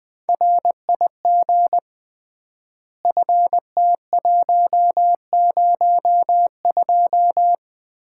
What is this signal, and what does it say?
Morse code, heard as a keyed tone, about 20 wpm, 700 Hz. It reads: RIG FT102
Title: Common QSO Phrases - 3x Word-Spacing - 20wpm